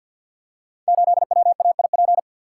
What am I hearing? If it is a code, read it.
Morse code, heard as a keyed tone, about 46 wpm, 700 Hz. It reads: 8WAIP
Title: Code Groups - 5 Letter:Numbers - 50wpm